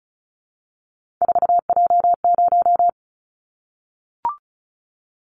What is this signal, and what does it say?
Morse code, heard as a keyed tone, about 34 wpm, 700 Hz. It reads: 4J0
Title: Callsign Prefixes for Top 100 DXCC - 35wpm